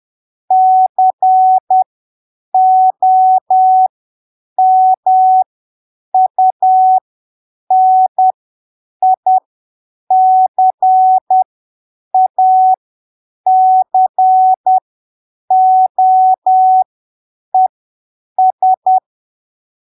Morse code, heard as a keyed tone, about 10 wpm, spaced as for 7 wpm, 750 Hz. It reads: COMUNICACOES
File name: arquivos / texto / texto 23 10ppm